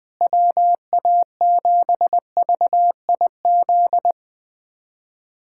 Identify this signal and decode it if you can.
Morse code, heard as a keyed tone, about 20 wpm, 700 Hz. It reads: WA7VIZ
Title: USA Callsign DE Callsign - 6 Characters - 20wpm